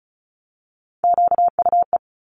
Morse code, heard as a keyed tone, about 36 wpm, 700 Hz. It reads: QUE
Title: Multi-letter Phonemes 35wpm